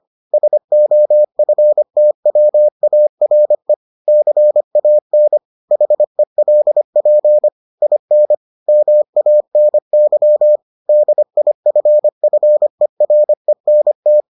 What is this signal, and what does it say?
Morse code, heard as a keyed tone, about 25 wpm, 600 Hz. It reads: SOFTWARE CAN HELP IN MANY DIFFERENT